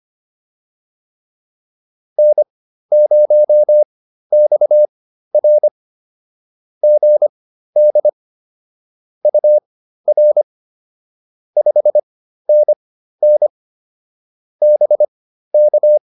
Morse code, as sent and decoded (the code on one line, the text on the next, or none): -. ----- -..- .-. / --. -.. / ..- .-. / ..... -. -. / -... -.-
N0XR GD UR 5NN BK